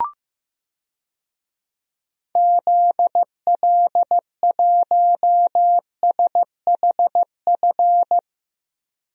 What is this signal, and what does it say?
Morse code, heard as a keyed tone, about 15 wpm, 700 Hz. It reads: ZL1SHF
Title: DXCC-100 Call Signs - 6 Characters - 15wpm